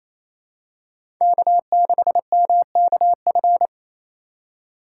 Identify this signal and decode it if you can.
Morse code, heard as a keyed tone, about 28 wpm, 700 Hz. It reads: K6MKF
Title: K1USN SST QSO Practice - 28wpm